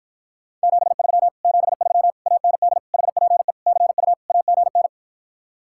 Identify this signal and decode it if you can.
Morse code, heard as a keyed tone, about 52 wpm, 700 Hz. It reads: 73 73 AND HPE CU AGN